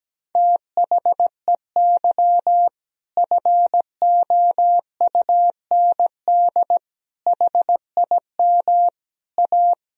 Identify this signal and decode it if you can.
Morse code, heard as a keyed tone, about 17 wpm, 700 Hz. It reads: THEY FOUND HIM A
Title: Sentences from Top 1000 Words - Review Part 1 - 17wpm